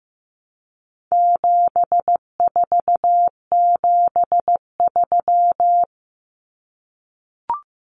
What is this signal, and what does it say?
Morse code, heard as a keyed tone, about 15 wpm, 700 Hz. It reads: 7473
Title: Common Cut Numbers - 4-Digit 15wpm